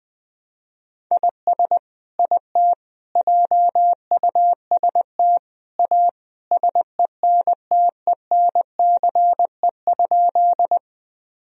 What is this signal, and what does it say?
Morse code, heard as a keyed tone, about 20 wpm, 700 Hz. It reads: IS IT JUST A SENTENCE?
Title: Sentences from Top 1000 Words - Review Part 1 - 20wpm